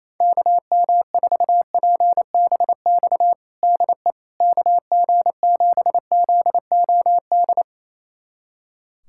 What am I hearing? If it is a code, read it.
Morse code, heard as a keyed tone, about 28 wpm, 700 Hz. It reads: KM4PBX DE KG7ZOD
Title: USA Callsign DE Callsign - 6 Characters - 28wpm